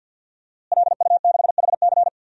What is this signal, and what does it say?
Morse code, heard as a keyed tone, about 53 wpm, 700 Hz. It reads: LU6HC
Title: DXCC-100 Call Signs - 5 Characters - Repeated 2x - 50wpm